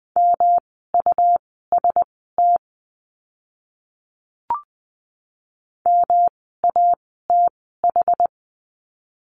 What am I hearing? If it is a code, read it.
Morse code, heard as a keyed tone, about 20 wpm, spaced as for 14 wpm, 700 Hz. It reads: MUST MATH
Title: 108 Lesson 18f - Letter m - Words 20wpm